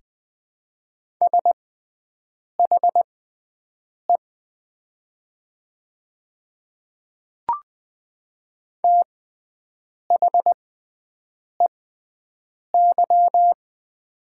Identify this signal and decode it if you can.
Morse code, heard as a keyed tone, about 20 wpm, spaced as for 7 wpm, 700 Hz. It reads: SHE THEY
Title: 6x Character Spacing - Top 200 Words - 20wpm